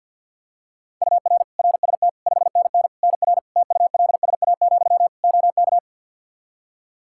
Morse code, heard as a keyed tone, about 53 wpm, 700 Hz. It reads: UR RST 5NN NR TULSA, OK
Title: Practice QSO - Exchange 1 - Beginning - 50wpm